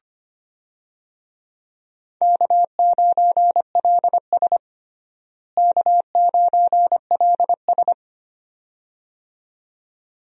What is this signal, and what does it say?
Morse code, heard as a keyed tone, about 25 wpm, 700 Hz. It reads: K9LS K9LS
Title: USA Call Signs - 4 Characters - Repeated 2x - 25wpm